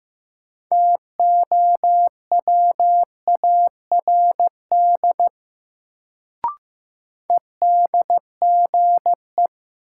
Morse code, heard as a keyed tone, about 15 wpm, 700 Hz. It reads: TOWARD EDGE
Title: Sentences from Top 1300 Words - New Word Drill Alt - 15wpm